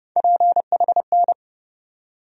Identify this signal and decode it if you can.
Morse code, heard as a keyed tone, about 30 wpm, 700 Hz. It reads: PHN
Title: USA Call Sign Postfixes - 3 Letters 30wpm